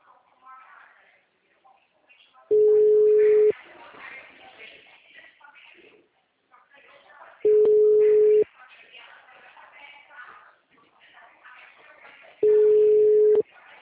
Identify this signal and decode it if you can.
Ringback tone (Europe, 425 Hz), with speech.